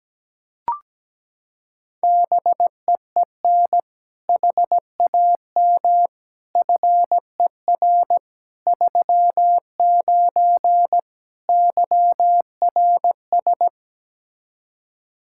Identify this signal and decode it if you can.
Morse code, heard as a keyed tone, about 17 wpm, 700 Hz. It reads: BEEN HAM FER 39 YRS
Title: K1USN SST QSO Practice 17wpm